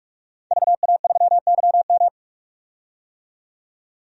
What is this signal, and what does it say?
Morse code, heard as a keyed tone, about 44 wpm, 700 Hz. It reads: VA3YM
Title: DXCC-100 Call Signs - 5 Characters - 45wpm